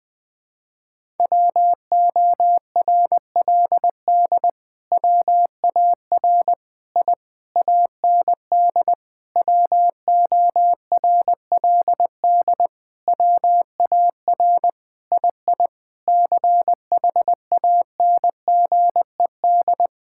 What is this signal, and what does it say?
Morse code, heard as a keyed tone, about 20 wpm, 700 Hz. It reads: WORLD WAR I AND WORLD WAR II CHANGED
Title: Sentences with 7-Word Phrases - 20wpm